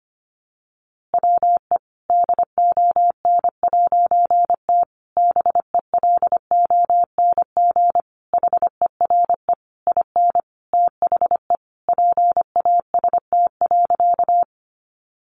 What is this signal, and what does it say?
Morse code, heard as a keyed tone, about 25 wpm, 700 Hz. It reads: WE DON'T BELONG HERE IN THE PAST.
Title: A Sound Of Thunder 25wpm